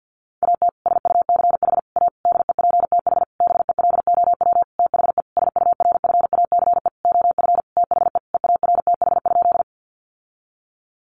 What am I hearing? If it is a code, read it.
Morse code, heard as a keyed tone, about 50 wpm, 700 Hz. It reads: AT SUCH A DEPTH BELOW THE SURFACE OF THE EARTH?
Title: Sentences from Top 1200 Words 50wpm